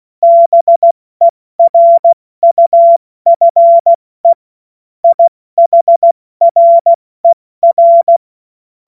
Morse code, heard as a keyed tone, about 16 wpm, 680 Hz. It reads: BERUFE IHRER
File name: Witz-14-Oktober-2025-70-F80